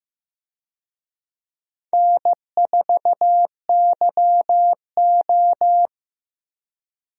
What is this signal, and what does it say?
Morse code, heard as a keyed tone, about 15 wpm, 700 Hz. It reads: N4YO
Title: USA Call Signs - 4 Characters - Repeated 3x - 15wpm